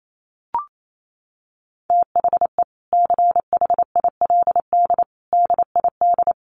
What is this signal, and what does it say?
Morse code, heard as a keyed tone, about 28 wpm, 700 Hz. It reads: THE CHILD DID
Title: Sentences from Top 800 Words 28wpm